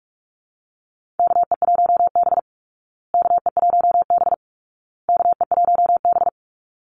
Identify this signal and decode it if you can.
Morse code, heard as a keyed tone, about 44 wpm, 700 Hz. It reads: KE1B KE1B KE1B
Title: Call Signs Repeated 2x - 45wpm